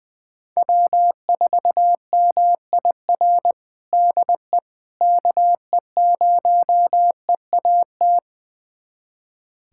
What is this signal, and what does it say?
Morse code, heard as a keyed tone, about 20 wpm, 700 Hz. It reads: W4MIR DE KE0EAT
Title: USA Callsign DE Callsign 20wpm